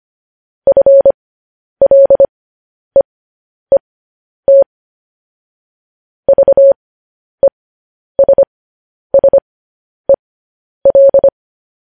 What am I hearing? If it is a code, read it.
Morse code, heard as a keyed tone, about 25 wpm, spaced as for 10 wpm, 550 Hz. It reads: FLEET VESSEL
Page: news-headlines-in-morse-code-at-10-wpm